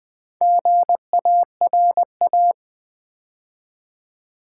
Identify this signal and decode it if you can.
Morse code, heard as a keyed tone, about 20 wpm, 700 Hz. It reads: GARA
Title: Build A Word - 2 Letters At A Time - 6-Letter Words - 20wpm